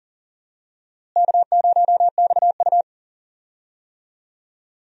Morse code, heard as a keyed tone, about 40 wpm, 700 Hz. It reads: K0XU
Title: USA Call Signs - 4 Characters - Repeated 2x - 40wpm